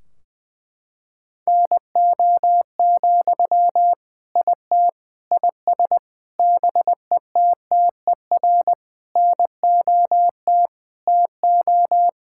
Morse code, heard as a keyed tone, about 20 wpm, 700 Hz. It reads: NO, IT IS BETTER NOT TO
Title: Sentences from Top 1600 Words 20wpm